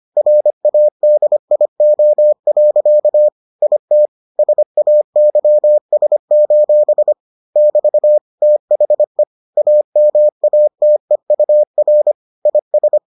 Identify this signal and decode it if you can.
Morse code, heard as a keyed tone, about 25 wpm, 600 Hz. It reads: RADIO. IT SAYS: <BT> THE AMATEUR IS